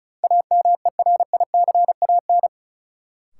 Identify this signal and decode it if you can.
Morse code, heard as a keyed tone, about 34 wpm, 700 Hz. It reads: AMERICAN
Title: Sets of 2 Words - New Word Drill - 35wpm